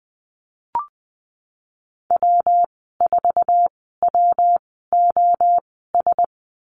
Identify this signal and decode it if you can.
Morse code, heard as a keyed tone, about 20 wpm, spaced as for 15 wpm, 700 Hz. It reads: W4WOS
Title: 122 Lesson 20g - Number 3 - Call Signs 20wpm